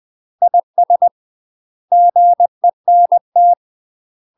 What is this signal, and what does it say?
Morse code, heard as a keyed tone, about 20 wpm, 700 Hz. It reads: IS GENT
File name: UK MASTER QSOs TRAINING/20WPM